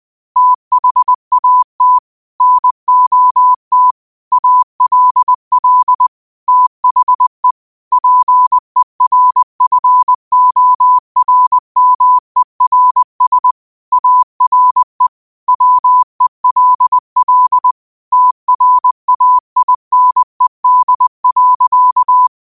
Morse code, heard as a keyed tone, about 20 wpm, 1000 Hz. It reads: THAT NOT ALL THE PERFORMERS ARE WELL TRAINED.
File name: qotdfiles / 20wpm